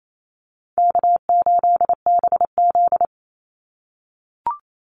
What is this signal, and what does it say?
Morse code, heard as a keyed tone, about 28 wpm, 700 Hz. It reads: K8BZ
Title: Easy Call Signs - 28wpm